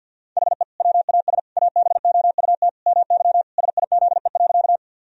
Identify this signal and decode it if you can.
Morse code, heard as a keyed tone, about 50 wpm, 700 Hz. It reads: HE WAS ABOUT MY SIZE.
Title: Sentences from Top 1000 Words - Review Part 2 - 50wpm